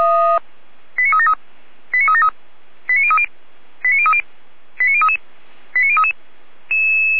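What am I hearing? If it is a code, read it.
Five-tone selective call (ZVEI), heard as quick runs of five tones, tones of about 70 ms. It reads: 89373 89373 89030 89030 89033 89033
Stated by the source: ZVEI five-tone call received by SDR.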